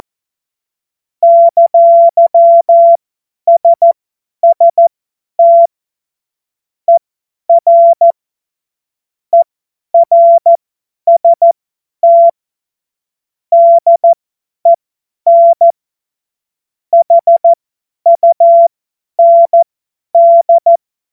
Morse code, heard as a keyed tone, about 14 wpm, spaced as for 10 wpm, 680 Hz. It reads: !SST ER ERST DEN HUND
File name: Morsekurs / Witze / Witz 02 April 2024 50 F70